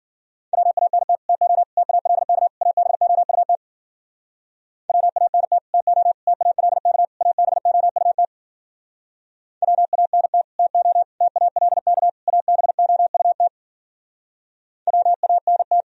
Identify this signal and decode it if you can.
Morse code, heard as a keyed tone, about 51 wpm, 700 Hz. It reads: WANT TO TALK ABOUT WANT TO TALK ABOUT WANT TO TALK ABOUT WANT